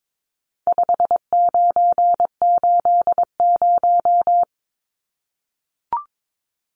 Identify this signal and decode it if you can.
Morse code, heard as a keyed tone, about 22 wpm, 700 Hz. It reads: 5980
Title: Numbers - 4-Digit 22wpm